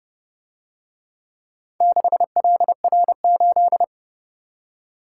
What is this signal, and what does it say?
Morse code, heard as a keyed tone, about 29 wpm, 700 Hz. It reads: 6LR8